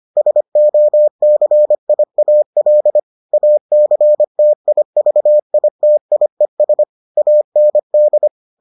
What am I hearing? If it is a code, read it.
Morse code, heard as a keyed tone, about 25 wpm, 600 Hz. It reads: SOCIAL ACTIVITIES AND